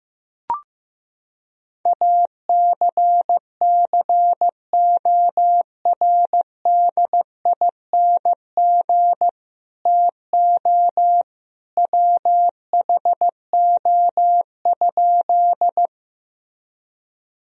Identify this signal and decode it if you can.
Morse code, heard as a keyed tone, about 15 wpm, 700 Hz. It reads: ACCORDING TO WHO?
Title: Sentences from Top 1000 Words - Review Part 4 - 15wpm